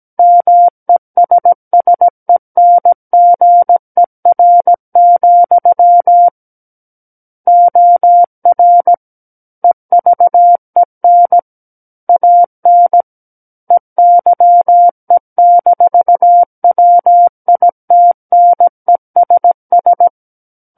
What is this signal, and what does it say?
Morse code, heard as a keyed tone, about 17 wpm, 700 Hz. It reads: MESSENGER, OR EVEN AN EYE-WITNESS
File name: Book One / 08-Chapter 8 - 17 WPM